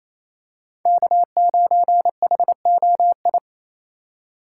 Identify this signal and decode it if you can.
Morse code, heard as a keyed tone, about 28 wpm, 700 Hz. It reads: K9HOI
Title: USA Call Signs - 5 Characters - 4x Character Spacing - 28wpm